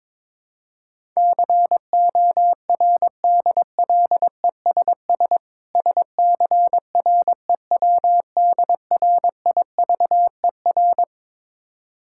Morse code, heard as a keyed tone, about 22 wpm, 690 Hz. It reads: CORDLESS SCREWDRIVER
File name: Difficult Words with Hints - 22wpm